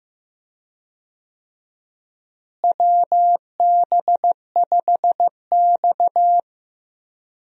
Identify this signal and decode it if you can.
Morse code, heard as a keyed tone, about 15 wpm, 700 Hz. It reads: WB5X
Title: USA Call Signs - 4 Characters - Repeated 3x - 15wpm